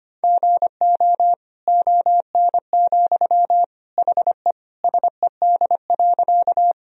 Morse code, heard as a keyed tone, about 25 wpm, 700 Hz. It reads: GO ON, HE SED.
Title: Sentences from Top 100 Words and Abbreviations 25wpm